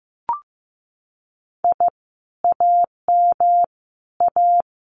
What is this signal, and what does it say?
Morse code, heard as a keyed tone, about 15 wpm, 700 Hz. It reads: I AM A
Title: Common QSO Phrases Encore 15wpm